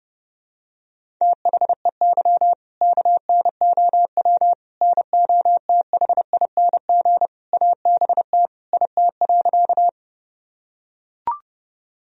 Morse code, heard as a keyed tone, about 32 wpm, 700 Hz. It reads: THEY KNOW NOTHING ABT IT.
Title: Sentences from Top 500 Words and Abbreviations 30wpm